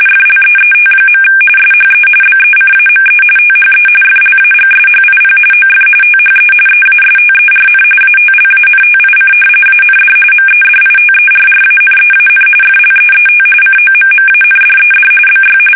RTTY, about 45.45 baud. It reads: FLEET. SHE BECAME THE FLAGSHIP OF THE EASTERN ATLANTIC AND MEDITERRANEAN FLEET COMMANDED BY
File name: fsk850 / battleshipnj